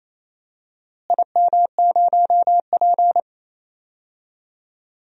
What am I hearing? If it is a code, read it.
Morse code, heard as a keyed tone, about 28 wpm, 700 Hz. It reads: IM0P
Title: DXCC-100 Call Signs - 4 Characters - Repeated 3x - 28wpm